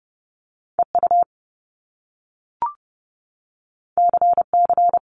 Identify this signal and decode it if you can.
Morse code, heard as a keyed tone, about 30 wpm, 700 Hz. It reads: EU CC
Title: Top Two Letter Combinations in Words 30wpm